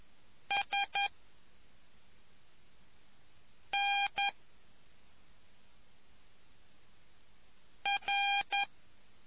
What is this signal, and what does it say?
Morse code, heard as a keyed tone, about 11 wpm, spaced as for 3 wpm, 800 Hz. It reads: SNR